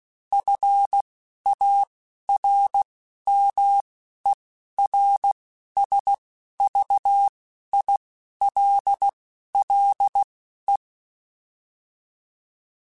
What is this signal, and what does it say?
Morse code, heard as a keyed tone, about 16 wpm, spaced as for 11 wpm, 800 Hz. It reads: FARMERSVILLE